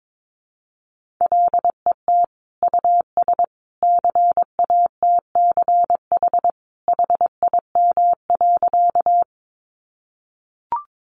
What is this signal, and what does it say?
Morse code, heard as a keyed tone, about 22 wpm, 700 Hz. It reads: LET US CATCH HIM.
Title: Sentences from Top 1500 Words 22wpm